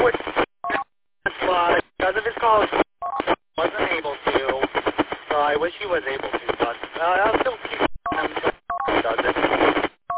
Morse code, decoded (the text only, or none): none